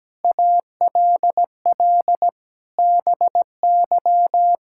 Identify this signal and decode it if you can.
Morse code, heard as a keyed tone, about 17 wpm, 700 Hz. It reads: ALL BY
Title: Sentences from Top 1000 Words - Review Part 5 - 17wpm